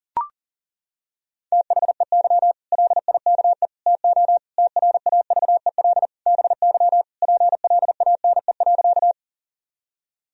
Morse code, heard as a keyed tone, about 40 wpm, 700 Hz. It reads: THEY LIKE TO TRAVEL BY PLANE.
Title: Sentences from Top 1000 Words - Review Part 3 - 40wpm